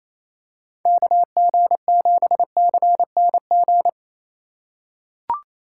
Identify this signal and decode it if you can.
Morse code, heard as a keyed tone, about 28 wpm, 700 Hz. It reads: KG7CNG
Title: USA Call Signs - 2x Character Spacing - 28wpm